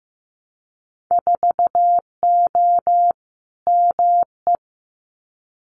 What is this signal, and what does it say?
Morse code, heard as a keyed tone, about 15 wpm, 700 Hz. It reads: 4O ME
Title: Winter Field Day Exchanges Encore 15wpm